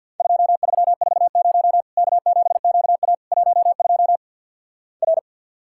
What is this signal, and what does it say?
Morse code, heard as a keyed tone, about 50 wpm, 700 Hz. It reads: 2340 K7QA 12 R
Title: Worked All Europe DX Contest - QTC Exchange with Repeats - 50wpm